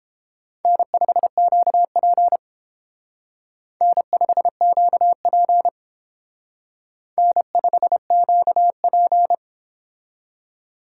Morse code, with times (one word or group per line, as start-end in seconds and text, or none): 0.7-2.4 N5QP
3.8-5.7 N5QP
7.2-9.3 N5QP